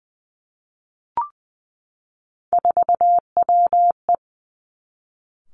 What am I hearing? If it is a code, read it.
Morse code, heard as a keyed tone, about 19 wpm, 700 Hz. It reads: 4WE
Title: Three Challenging Letter-Number Combinations 20wpm